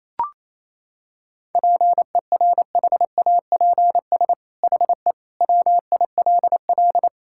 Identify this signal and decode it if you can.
Morse code, heard as a keyed tone, about 28 wpm, 700 Hz. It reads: PERHAPS HE WILL
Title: Sentences from Top 1600 Words 28wpm